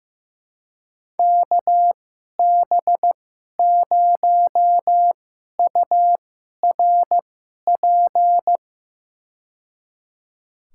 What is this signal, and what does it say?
Morse code, heard as a keyed tone, about 16 wpm, spaced as for 11 wpm, 700 Hz. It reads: KB0URP
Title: USA Call Signs - 2x Character Spacing - 15wpm